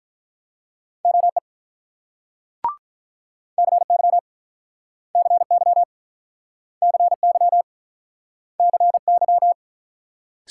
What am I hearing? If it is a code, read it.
Morse code, heard as a keyed tone, about 46 wpm, 690 Hz. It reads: OE CY CY CY CY